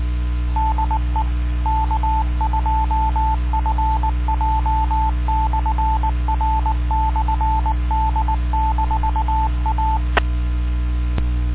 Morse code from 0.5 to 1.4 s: DE